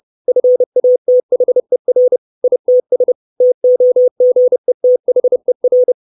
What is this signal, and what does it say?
Morse code, heard as a keyed tone, about 30 wpm, 500 Hz. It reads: FATHER ITS TOGETHER
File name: 632 Words 30WPM